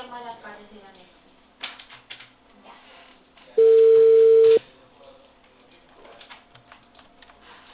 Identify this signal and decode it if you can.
Ringback tone (China, 450 Hz), with speech.